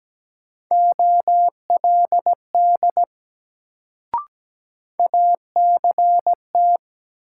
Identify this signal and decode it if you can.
Morse code, heard as a keyed tone, about 17 wpm, 700 Hz. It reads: OLD ACT
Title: Sentences from Top 200 Words - New Word Drill - 17wpm